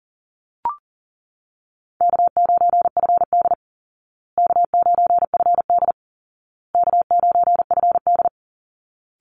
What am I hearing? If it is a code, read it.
Morse code, heard as a keyed tone, about 40 wpm, 700 Hz. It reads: K9FD K9FD K9FD
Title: Call Signs Repeated 2x - 40wpm